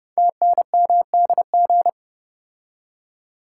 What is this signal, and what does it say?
Morse code, heard as a keyed tone, about 30 wpm, 700 Hz. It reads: TNMDG
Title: Code Groups - 5 Letters - 30wpm